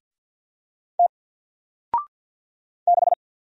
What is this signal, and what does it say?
Morse code, heard as a keyed tone, about 48 wpm, 700 Hz. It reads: T 6
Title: Single Letter-Number Alt 50wpm